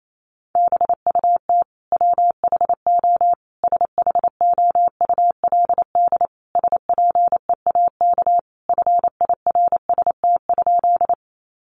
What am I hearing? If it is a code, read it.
Morse code, heard as a keyed tone, about 28 wpm, 700 Hz. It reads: BUT WHO SHOULD SPEAK FIRST?
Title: Sentences from Top 1100 Words 28wpm